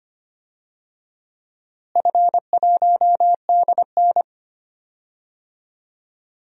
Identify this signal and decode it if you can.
Morse code, heard as a keyed tone, about 25 wpm, 700 Hz. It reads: F1DN
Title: DXCC-100 Call Signs - 4 Characters - Repeated 3x - 25wpm